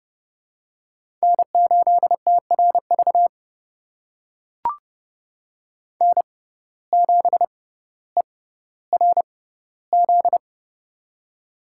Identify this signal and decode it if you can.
Morse code, heard as a keyed tone, about 31 wpm, spaced as for 16 wpm, 700 Hz. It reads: N8TRV N7ERZ
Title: USA Call Signs - 5 Characters - 6x Character Spacing - 30wpm